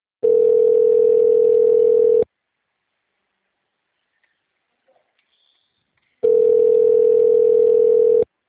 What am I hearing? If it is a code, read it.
Ringback tone (North America), 440 + 480 Hz.
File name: out- -2016- - - .26